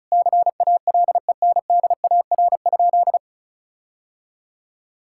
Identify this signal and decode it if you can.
Morse code, heard as a keyed tone, about 34 wpm, 700 Hz. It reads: CALENDAR?
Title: Sentences from Top 1600 Words 35wpm